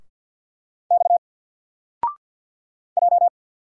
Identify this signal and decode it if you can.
Morse code, heard as a keyed tone, about 50 wpm, 700 Hz. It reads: X J
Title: Single Letter-Number 50wpm